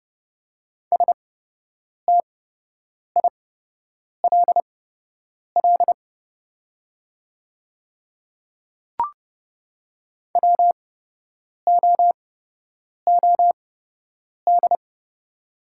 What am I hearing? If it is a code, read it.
Morse code, heard as a keyed tone, about 31 wpm, spaced as for 8 wpm, 700 Hz. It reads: STILL WOOD
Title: 8x Character Spacing - Top 300 Words - 30wpm